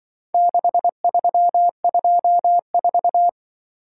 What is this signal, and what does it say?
Morse code, heard as a keyed tone, about 25 wpm, 700 Hz. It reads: 6324